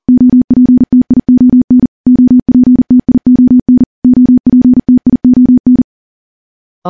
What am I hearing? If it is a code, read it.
Morse code, heard as a keyed tone, about 40 wpm, 250 Hz. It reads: OPTION OPTION OPTION
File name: morse / morse-training-40WPM